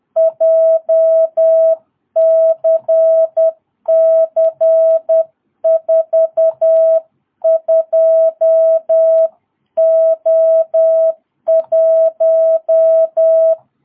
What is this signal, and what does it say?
Morse code, heard as a keyed tone, about 10 wpm, 660 Hz. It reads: JCC42O1
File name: jh8cbh / 2022cw